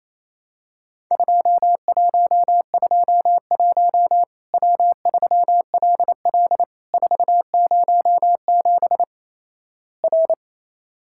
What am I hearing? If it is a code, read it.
Morse code, heard as a keyed tone, about 28 wpm, 700 Hz. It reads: 2121 W3LL 407 R
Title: Worked All Europe DX Contest - QTC Exchange - 28wpm